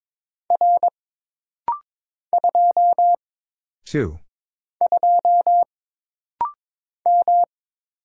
Morse code, heard as a keyed tone, about 22 wpm, 700 Hz. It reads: R 2 2 M
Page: This rapid-fire timing is geared to push your ICR (Instant Character Recognition) skills.